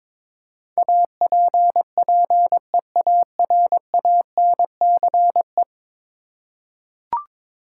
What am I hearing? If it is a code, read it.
Morse code, heard as a keyed tone, about 22 wpm, 700 Hz. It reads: APPEARANCE
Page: morse-code-ninja-practice-22wpm